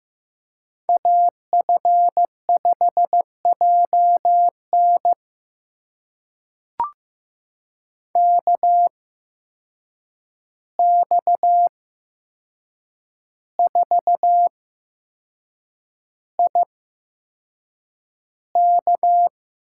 Morse code, heard as a keyed tone, about 15 wpm, spaced as for 6 wpm, 700 Hz. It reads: AF5JN KX4IK
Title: USA Call Signs - 5 Characters - 8x Character Spacing - 15wpm